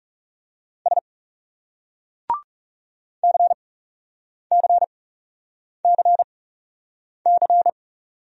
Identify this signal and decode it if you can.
Morse code, heard as a keyed tone, about 37 wpm, spaced as for 8 wpm, 700 Hz. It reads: S CCCC
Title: Speed-Racing Single Letters 30wpm